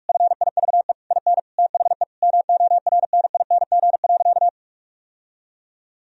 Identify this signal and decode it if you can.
Morse code, heard as a keyed tone, about 46 wpm, 700 Hz. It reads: FIVE IN THE MORNING.
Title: Sentences from Top 1000 Words - Review Part 2 - 45wpm